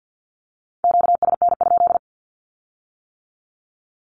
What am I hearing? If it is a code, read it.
Morse code, heard as a keyed tone, about 50 wpm, 700 Hz. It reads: QSN?
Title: Q-Codes with Spoken Meaning - 50wpm